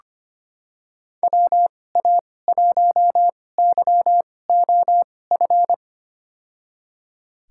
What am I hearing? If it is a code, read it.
Morse code, heard as a keyed tone, about 25 wpm, spaced as for 18 wpm, 700 Hz. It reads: WA1YOF
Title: USA Call Signs - 6 Characters - 2x Character Spacing - 25wpm